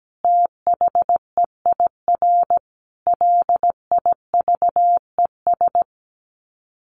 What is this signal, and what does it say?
Morse code, heard as a keyed tone, about 17 wpm, 700 Hz. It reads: THEIR LIVES
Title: Sets of 5 Words Encore 17wpm